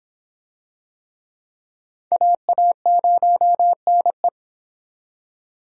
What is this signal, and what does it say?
Morse code, heard as a keyed tone, about 26 wpm, 700 Hz. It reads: AA0NE